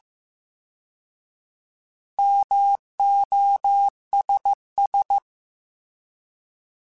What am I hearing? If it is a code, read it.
Morse code, heard as a keyed tone, about 15 wpm, 800 Hz. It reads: MOSS
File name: Alphabetical / Alphabetical mp3s / Ms @13